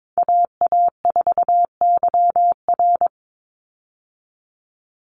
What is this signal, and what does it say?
Morse code, heard as a keyed tone, about 22 wpm, 700 Hz. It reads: AA4YR
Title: USA Call Signs 22wpm